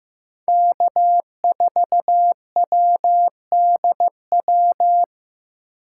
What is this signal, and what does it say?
Morse code, heard as a keyed tone, about 15 wpm, 700 Hz. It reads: K4WDW
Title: USA Call Signs - 5 Characters - 6x Character Spacing - 15wpm